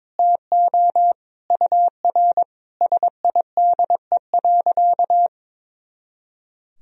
Morse code, heard as a keyed tone, about 22 wpm, 700 Hz. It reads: TO UR SIDE.
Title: Sentences from Top 400 Words and Abbreviations 22wpm